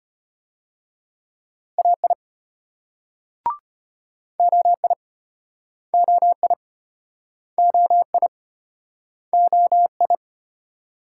Morse code, heard as a keyed tone, about 33 wpm, 700 Hz. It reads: AI OI OI OI OI